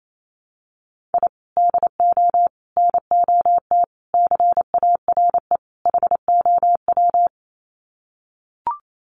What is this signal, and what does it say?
Morse code, heard as a keyed tone, about 28 wpm, 700 Hz. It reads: I DO NOT CARE HOW
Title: Sets of 5 Words Encore2 28wpm